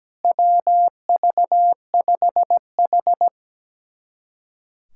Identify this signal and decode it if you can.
Morse code, heard as a keyed tone, about 17 wpm, 700 Hz. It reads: WV5H
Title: USA Callsign DE Callsign - 4 Characters - 17wpm